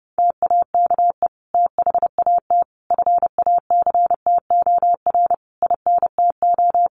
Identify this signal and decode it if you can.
Morse code, heard as a keyed tone, about 30 wpm, 700 Hz. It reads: TAKE THAT FACTOR INTO
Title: Sentences from Top 1300 Words 30wpm